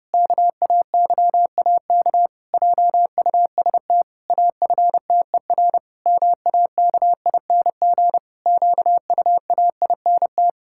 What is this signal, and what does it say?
Morse code, heard as a keyed tone, about 32 wpm, 700 Hz. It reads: KAYAK JUST AFTER MAKING QUAINT
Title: Pangram Challenge v2 - 30wpm